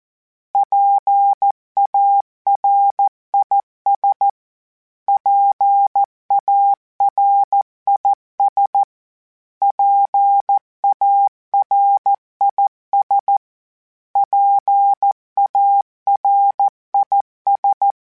Morse code, heard as a keyed tone, about 14 wpm, 800 Hz. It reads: PARIS PARIS PARIS PARIS
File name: PARIS Calibration / PARIS mp3s / PARIS@13